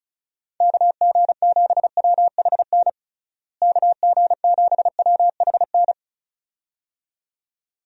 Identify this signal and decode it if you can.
Morse code, heard as a keyed tone, about 36 wpm, 700 Hz. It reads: KG7WHN KG7WHN
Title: USA Call Signs - 6 Characters - Repeated 2x - 35wpm